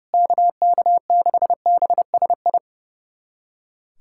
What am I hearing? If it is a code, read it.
Morse code, heard as a keyed tone, about 30 wpm, 700 Hz. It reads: KK6BSI
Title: USA Callsign DE Callsign - 6 Characters - 30wpm